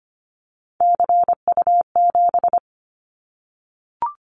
Morse code, heard as a keyed tone, about 25 wpm, 700 Hz. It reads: CU7
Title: Callsign Prefixes for Top 50 DXCC - 25wpm